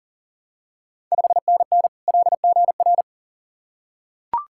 Morse code, heard as a keyed tone, about 40 wpm, 690 Hz. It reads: 5NN LGR
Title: Ontario QSO Party Exchange 40wpm